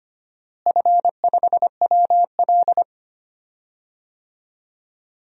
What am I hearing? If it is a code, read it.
Morse code, heard as a keyed tone, about 25 wpm, 700 Hz. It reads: F5WL
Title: DXCC-100 Call Signs - 4 Characters - Repeated 2x - 25wpm